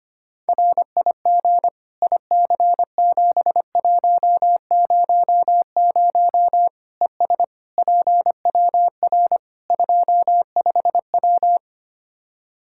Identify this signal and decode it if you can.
Morse code, heard as a keyed tone, about 25 wpm, 700 Hz. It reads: RIG IC7100 ES PWR 25W
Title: K1USN SST QSO Practice 25wpm